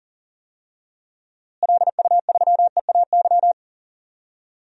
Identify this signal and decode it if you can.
Morse code, heard as a keyed tone, about 39 wpm, 700 Hz. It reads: LU3EAY
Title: DXCC-100 Call Signs - 6 Characters - Repeated 3x - 40wpm